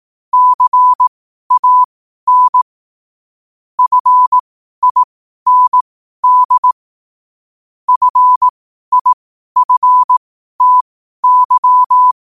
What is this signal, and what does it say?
Morse code, heard as a keyed tone, about 18 wpm, spaced as for 12 wpm, 1000 Hz. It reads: CAN FIND FIFTY
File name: qotdfiles / 12wpm